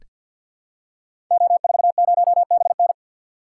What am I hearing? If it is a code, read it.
Morse code, heard as a keyed tone, about 46 wpm, 700 Hz. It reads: O40BN
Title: Code Groups - 5 Letter:Numbers - 50wpm